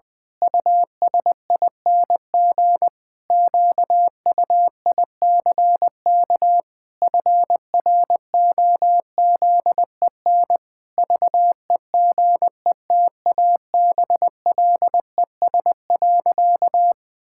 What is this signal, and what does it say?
Morse code, heard as a keyed tone, about 21 wpm, 700 Hz. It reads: USING QUICK FROZEN VEGETABLES.
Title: Pangram Challenge v2 - 20wpm